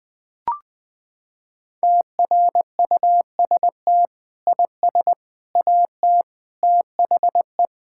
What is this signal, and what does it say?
Morse code, heard as a keyed tone, about 21 wpm, 700 Hz. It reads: TRUST IS AT THE
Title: Sentences with 6-Word Phrases - 20wpm